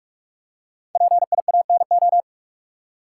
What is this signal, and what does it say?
Morse code, heard as a keyed tone, about 46 wpm, 700 Hz. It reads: PIANO
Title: Five Letter Words - 45wpm